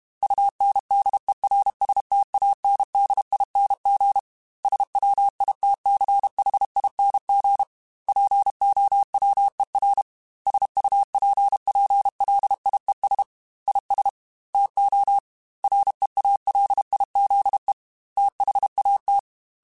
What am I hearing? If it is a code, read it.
Morse code, heard as a keyed tone, about 32 wpm, 800 Hz. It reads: UNDERSTANDING SWITCHING POWER SUPPLIES IS TO REALIZE THAT
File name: Technical Subjects / Technical Subjects-mp3s / Sw Pwr Sup @30